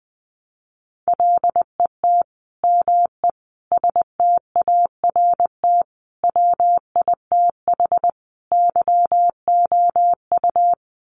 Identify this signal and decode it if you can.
Morse code, heard as a keyed tone, about 20 wpm, 700 Hz. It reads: LET ME START WITH YOU
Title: Sets of 5 Words Encore 20wpm